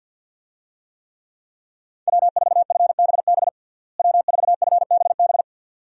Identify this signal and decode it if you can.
Morse code, heard as a keyed tone, about 48 wpm, 700 Hz. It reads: W4FBB W4FBB
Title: USA Call Signs - 5 Characters - Repeated 2x - 50wpm